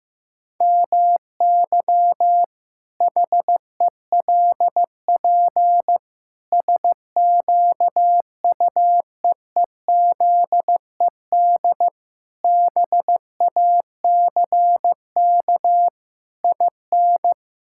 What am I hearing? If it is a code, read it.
Morse code, heard as a keyed tone, about 15 wpm, 700 Hz. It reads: MY HELP SQUEEZED BACK IN
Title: Pangram Challenge v2 - 15wpm